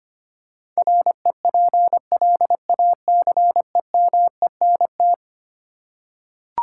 Morse code, heard as a keyed tone, about 25 wpm, 700 Hz. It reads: REPLACEMENT
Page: morse-code-ninja-practice-25wpm